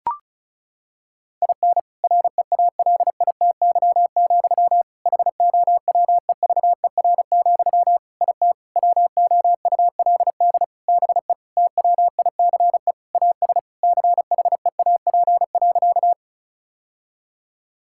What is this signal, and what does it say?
Morse code, heard as a keyed tone, about 34 wpm, 700 Hz. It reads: IN REALITY, HOWEVER, IT WOULD BE TWICE AS CHEAP.
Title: Sentences from Top 1600 Words 35wpm